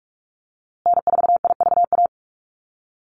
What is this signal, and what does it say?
Morse code, heard as a keyed tone, about 44 wpm, 700 Hz. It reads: N4IVA
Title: USA Call Signs Repeated 2x 45wpm